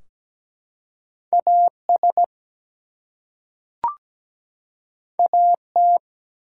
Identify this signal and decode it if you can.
Morse code, heard as a keyed tone, about 17 wpm, 700 Hz. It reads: AS AT
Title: Two Letter Words - 17wpm